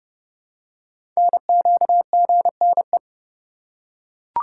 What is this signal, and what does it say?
Morse code, heard as a keyed tone, about 30 wpm, 700 Hz. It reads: NQGNE
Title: Code Groups - 5 Letters - 30wpm